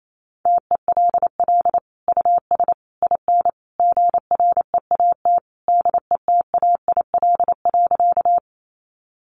Morse code, heard as a keyed tone, about 28 wpm, 700 Hz. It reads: TELL US IN GREAT DETAIL.
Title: Sentences from Top 1500 Words 28wpm